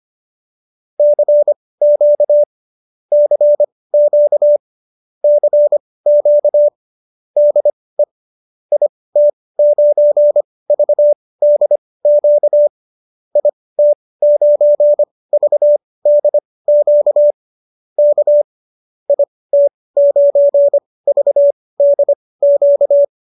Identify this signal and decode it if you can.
Morse code, heard as a keyed tone, about 25 wpm, spaced as for 18 wpm, 590 Hz. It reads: CQ CQ CQ DE IT9VDQ IT9VDQ K IT9VDQ